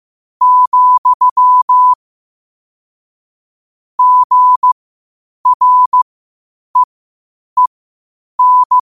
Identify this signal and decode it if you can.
Morse code, heard as a keyed tone, about 15 wpm, spaced as for 8 wpm, 1000 Hz. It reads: , GREEN
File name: qotdfiles / 8wpm